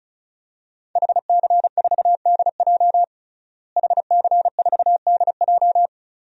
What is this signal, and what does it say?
Morse code, heard as a keyed tone, about 34 wpm, 700 Hz. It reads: HC4DJ HC4DJ
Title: DXCC-100 Call Signs - 5 Characters - Repeated 3x - 35wpm